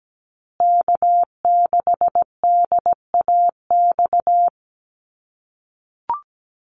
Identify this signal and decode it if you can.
Morse code, heard as a keyed tone, about 17 wpm, 700 Hz. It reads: K6DAX
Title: USA Call Signs Repeated 3x 17wpm